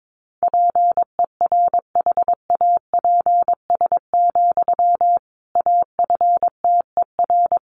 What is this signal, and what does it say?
Morse code, heard as a keyed tone, about 22 wpm, 700 Hz. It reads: PERHAPS, AFTER
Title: Sentences from Top 500 Words 22wpm